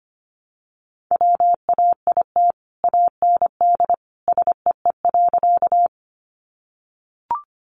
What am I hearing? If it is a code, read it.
Morse code, heard as a keyed tone, about 25 wpm, 700 Hz. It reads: WAIT AND SEE.
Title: Sentences from Top 500 Words 25wpm